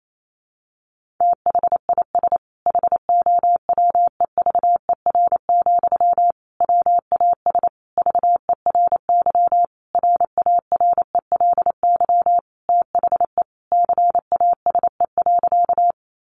Morse code, heard as a keyed tone, about 28 wpm, 700 Hz. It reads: THIS HOWEVER, WAS VERY RARELY THE CASE.
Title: Sentences from Top 1400 Words 28wpm